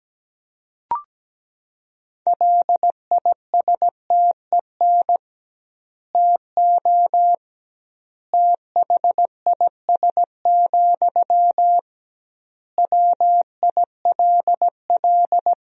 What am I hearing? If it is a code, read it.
Morse code, heard as a keyed tone, about 17 wpm, 700 Hz. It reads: LISTEN TO THIS, WILL
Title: 2x Word-Spacing - Sentences from Top 400 Words - 17wpm